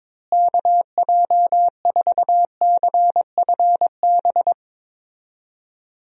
Morse code, heard as a keyed tone, about 22 wpm, 700 Hz. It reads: KJ4CFB
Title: USA Call Signs - 6 Characters - Repeated 2x - 22wpm